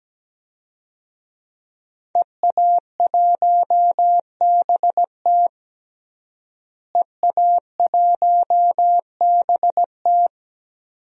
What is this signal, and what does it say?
Morse code, heard as a keyed tone, about 17 wpm, 700 Hz. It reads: EA1BT EA1BT
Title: DXCC-100 Call Signs - 5 Characters - Repeated 2x - 17wpm